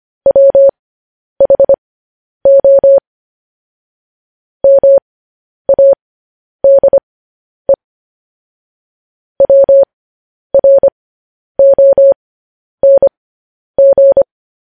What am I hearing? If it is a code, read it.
Morse code, heard as a keyed tone, about 25 wpm, spaced as for 10 wpm, 550 Hz. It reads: WHO MADE WRONG